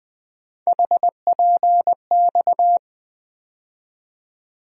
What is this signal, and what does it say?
Morse code, heard as a keyed tone, about 21 wpm, 700 Hz. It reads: HPX
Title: USA Call Sign Postfixes - 3 Letters 20wpm